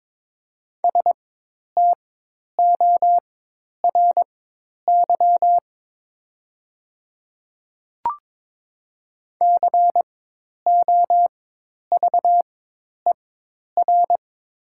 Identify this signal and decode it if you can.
Morse code, heard as a keyed tone, about 22 wpm, spaced as for 10 wpm, 700 Hz. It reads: STORY COVER
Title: 4x Character Spacing - Top 300 Words - 22wpm